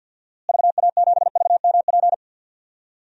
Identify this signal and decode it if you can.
Morse code, heard as a keyed tone, about 50 wpm, 700 Hz. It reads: VA7VMP
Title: DXCC-100 Call Signs - 6 Characters - Repeated 2x - 50wpm